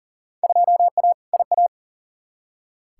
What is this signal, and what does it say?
Morse code, heard as a keyed tone, about 40 wpm, 700 Hz. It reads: 2A IA
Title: Field Day Exchanges - 40wpm